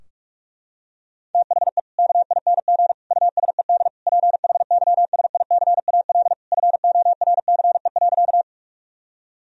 Morse code, heard as a keyed tone, about 43 wpm, 700 Hz. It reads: THE KING USED PHYSICAL FORCE.
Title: Sentences from Top 1000 Words - Review Part 4 - 45wpm